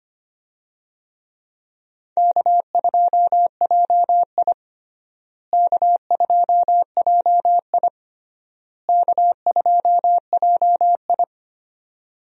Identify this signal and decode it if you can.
Morse code, heard as a keyed tone, about 25 wpm, 700 Hz. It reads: K2JI K2JI K2JI
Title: USA Call Signs - 4 Characters - Repeated 3x - 25wpm